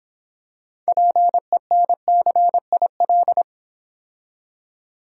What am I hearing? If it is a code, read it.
Morse code, heard as a keyed tone, about 26 wpm, 700 Hz. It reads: PENCIL